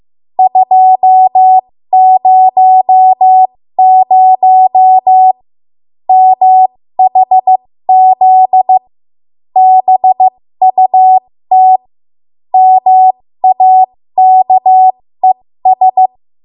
Morse code, heard as a keyed tone, about 15 wpm, spaced as for 13 wpm, 750 Hz. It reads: 200 MHZ BUT MAKES
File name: Morse / 13WPM